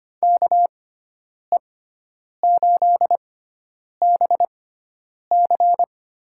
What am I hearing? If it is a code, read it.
Morse code, heard as a keyed tone, about 25 wpm, spaced as for 9 wpm, 700 Hz. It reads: KE8BC
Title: USA Call Signs - 6 Characters - 6x Character Spacing - 25wpm